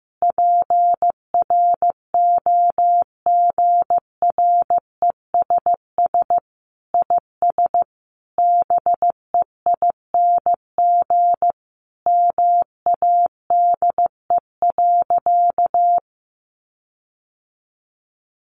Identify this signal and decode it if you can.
Morse code, heard as a keyed tone, about 15 wpm, 700 Hz. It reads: PROGRESS IS BEING MADE.
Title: Sentences from Top 1400 Words 15wpm